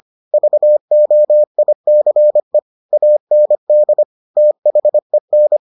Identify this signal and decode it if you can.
Morse code, heard as a keyed tone, about 25 wpm, 600 Hz. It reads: VOICE AND THEN